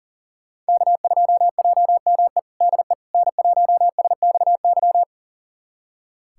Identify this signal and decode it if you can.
Morse code, heard as a keyed tone, about 37 wpm, 700 Hz. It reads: K2JME DE N1SXY
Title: USA Callsign DE Callsign - 5 Characters - 40wpm